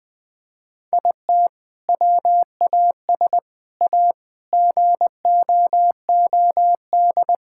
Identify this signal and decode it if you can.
Morse code, heard as a keyed tone, about 20 wpm, 700 Hz. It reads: IT WAS A GOOD
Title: Sentences with 2-Word Phrases - 20wpm